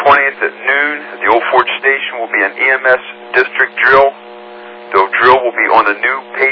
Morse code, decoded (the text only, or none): none